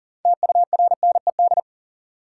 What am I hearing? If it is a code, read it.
Morse code, heard as a keyed tone, about 40 wpm, 700 Hz. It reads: TURNED
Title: Sets of 4 Words - New Word Drill - 40wpm